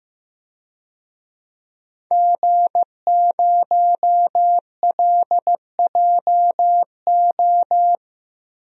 Morse code, heard as a keyed tone, about 15 wpm, 700 Hz. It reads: G0LJO
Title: DXCC-100 Call Signs - 5 Characters - 15wpm